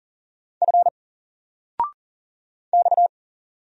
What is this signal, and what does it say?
Morse code, heard as a keyed tone, about 38 wpm, 700 Hz. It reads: F X
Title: Single Character (Letter:Number:Punctuation:Prosign) 40wpm